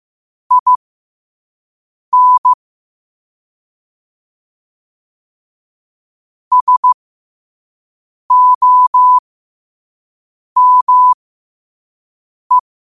Morse code, heard as a keyed tone, about 15 wpm, spaced as for 5 wpm, 1000 Hz. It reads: IN SOME
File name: 5wpm